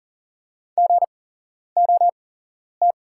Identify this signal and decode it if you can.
Morse code, heard as a keyed tone, about 39 wpm, spaced as for 11 wpm, 700 Hz. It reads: GOT
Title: 8x Character Spacing - Top 300 Words - 40wpm